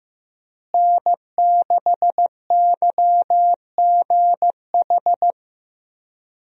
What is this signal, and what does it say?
Morse code, heard as a keyed tone, about 15 wpm, 700 Hz. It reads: N6YGH
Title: USA Call Signs - 5 Characters - Repeated 3x - 15wpm